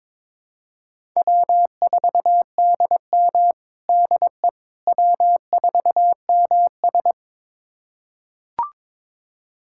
Morse code, heard as a keyed tone, about 22 wpm, 700 Hz. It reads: W4DM DE W4MS
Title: USA Callsign DE Callsign - 4 Characters - 22wpm